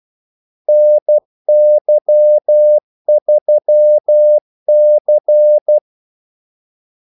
Morse code, heard as a keyed tone, about 12 wpm, 600 Hz. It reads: NY3C
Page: 2 ekstra ordmellemrum mellem kaldesignalerne